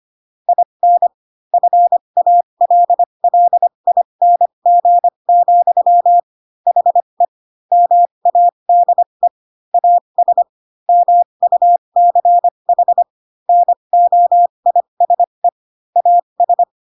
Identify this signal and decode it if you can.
Morse code, heard as a keyed tone, about 25 wpm, spaced as for 22 wpm, 700 Hz. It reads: IN FALLING, HE MADE AS MUCH NOISE AS
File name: Pinocchio / Pinocchio ch7 18 25 ik1hge